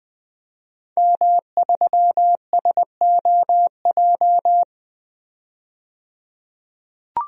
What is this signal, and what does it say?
Morse code, heard as a keyed tone, about 20 wpm, 700 Hz. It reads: M3SOJ
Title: DXCC-100 Call Signs - 5 Characters - 20wpm